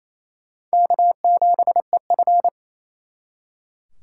Morse code, heard as a keyed tone, about 34 wpm, 700 Hz. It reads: K7EF
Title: Speed Racing - USA 1x2 Call Signs 28wpm